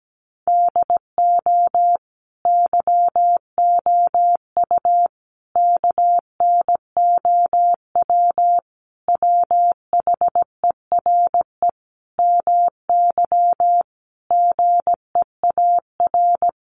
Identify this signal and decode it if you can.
Morse code, heard as a keyed tone, about 17 wpm, 700 Hz. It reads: DO YOU KNOW WHERE MY GEAR
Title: Sentences from Top 1400 Words 17wpm